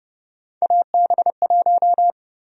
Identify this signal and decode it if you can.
Morse code, heard as a keyed tone, about 32 wpm, 700 Hz. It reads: AB1
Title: USA Call Sign Prefixes - Repeated 2x - 30wpm